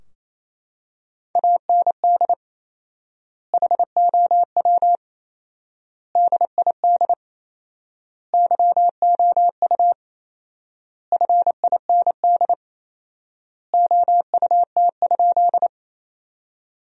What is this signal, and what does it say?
Morse code, heard as a keyed tone, about 28 wpm, 700 Hz. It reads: AND HOW DID YOU FIND OUT?
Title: 4x Word-Spacing - Sentences from Top 100 Words - 28wpm